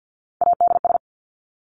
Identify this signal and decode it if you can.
Morse code, heard as a keyed tone, about 50 wpm, 700 Hz. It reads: ADS
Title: Abbreviations Repeated 3x - 50wpm